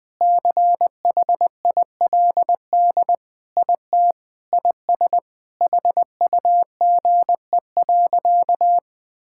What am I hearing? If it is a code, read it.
Morse code, heard as a keyed tone, about 20 wpm, 700 Hz. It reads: CHILD IT IS HUGE.
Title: Sentences with 5-Word Phrases - 20wpm